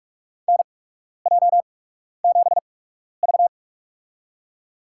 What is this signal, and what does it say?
Morse code, heard as a keyed tone, about 42 wpm, spaced as for 12 wpm, 690 Hz. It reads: NJ7V
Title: USA Call Signs - 8x Character Spacing - 45wpm